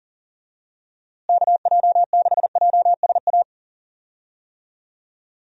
Morse code, heard as a keyed tone, about 40 wpm, 700 Hz. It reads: KJ6JSA
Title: USA Call Signs - 6 Characters - Repeated 3x - 40wpm